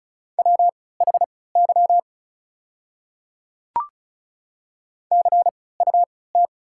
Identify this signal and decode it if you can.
Morse code, heard as a keyed tone, about 36 wpm, spaced as for 20 wpm, 690 Hz. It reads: WHY CUT
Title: 3x Character Spacing - Top 300 Words - 35wpm